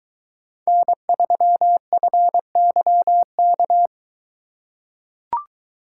Morse code, heard as a keyed tone, about 22 wpm, 700 Hz. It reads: N3FYK